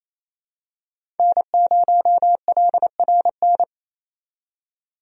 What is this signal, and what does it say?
Morse code, heard as a keyed tone, about 28 wpm, 700 Hz. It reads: N0LRN
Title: USA Call Signs - 5 Characters - 28wpm